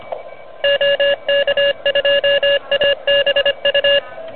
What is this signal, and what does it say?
Morse code, heard as a keyed tone, about 25 wpm, 2800 Hz, with noise. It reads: OK2ABU
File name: truSDX / NZ contest